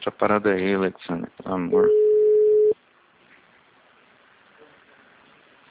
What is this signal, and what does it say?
Ringback tone (Europe, 425 Hz), with speech.
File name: out- -2021- - - .3998